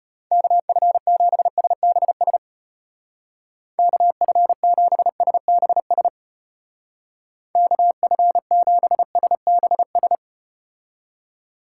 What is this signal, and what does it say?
Morse code, heard as a keyed tone, about 33 wpm, 700 Hz. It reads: KF7SBS KF7SBS KF7SBS